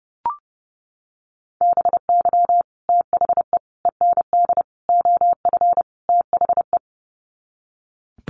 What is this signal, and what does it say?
Morse code, heard as a keyed tone, about 30 wpm, 700 Hz. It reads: BY THE END OF THE
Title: Sets of 5 Words 30wpm